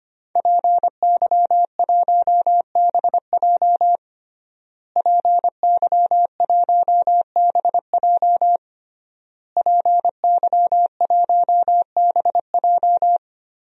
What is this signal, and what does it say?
Morse code, heard as a keyed tone, about 25 wpm, 700 Hz. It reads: PY1BJ PY1BJ PY1BJ
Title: DXCC-100 Call Signs - 5 Characters - Repeated 3x - 25wpm